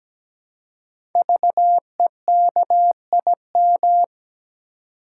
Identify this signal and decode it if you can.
Morse code, heard as a keyed tone, about 17 wpm, 700 Hz. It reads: VEKIM
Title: Code Groups - 5 Letters - 17wpm